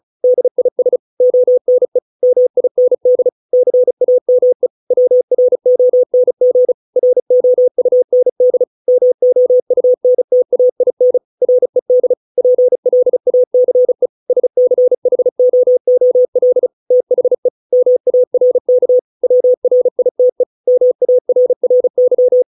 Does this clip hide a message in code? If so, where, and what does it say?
Yes: Morse code from 0.2 to 22.4 s, heard as a keyed tone, about 34 wpm, 500 Hz. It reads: DIS ONE MIND CAME WRONG ROUND MOUNTAIN RED PLACE SCHOOL THE MARK WRITE MARRY